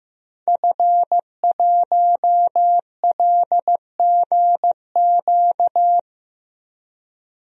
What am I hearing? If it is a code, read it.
Morse code, heard as a keyed tone, about 15 wpm, 700 Hz. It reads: F1LGQ
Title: DXCC-100 Call Signs - 5 Characters - Repeated 2x - 15wpm